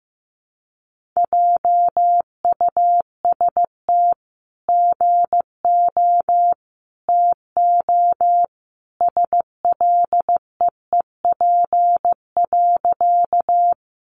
Morse code, heard as a keyed tone, about 15 wpm, 700 Hz. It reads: JUST GO TO SLEEP.
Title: Sentences from Top 500 Words 15wpm